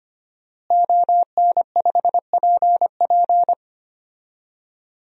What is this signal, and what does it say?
Morse code, heard as a keyed tone, about 25 wpm, 700 Hz. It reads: ON5PP
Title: DXCC-100 Call Signs - 5 Characters - Repeated 2x - 25wpm